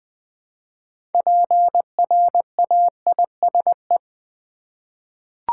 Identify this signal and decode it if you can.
Morse code, heard as a keyed tone, about 20 wpm, 700 Hz. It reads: PRAISE
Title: 6 Letter Words - First 3 Letters Given - 20wpm